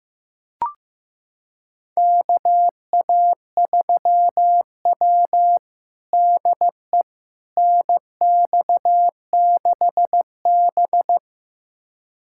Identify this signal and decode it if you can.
Morse code, heard as a keyed tone, about 15 wpm, 700 Hz. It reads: KA3W DE NX6B
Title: USA Callsign DE Callsign - 4 Characters - 15wpm